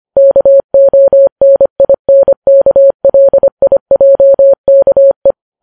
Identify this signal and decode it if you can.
Morse code, heard as a keyed tone, about 25 wpm, 550 Hz. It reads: KONINKLIJKE